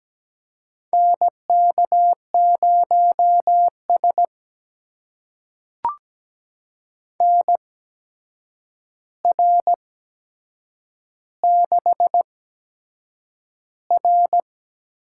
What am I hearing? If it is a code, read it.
Morse code, heard as a keyed tone, about 17 wpm, spaced as for 7 wpm, 700 Hz. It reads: NK0S NR6R
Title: USA Call Signs - 4 Characters - 8x Character Spacing - 17wpm